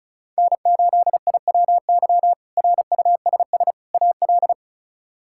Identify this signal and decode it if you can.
Morse code, heard as a keyed tone, about 34 wpm, 700 Hz. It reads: N8IWY RUSS AL
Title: K1USN SST QSO Practice - 35wpm